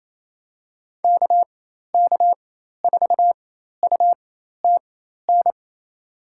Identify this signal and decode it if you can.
Morse code, heard as a keyed tone, about 27 wpm, spaced as for 13 wpm, 700 Hz. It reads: KK4UTN
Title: USA Call Signs - 4x Character Spacing - 28wpm